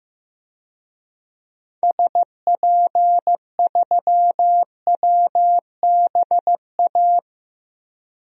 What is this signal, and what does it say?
Morse code, heard as a keyed tone, about 15 wpm, 700 Hz. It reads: SP3WBA
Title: DXCC-100 Call Signs - 6 Characters - Repeated 2x - 15wpm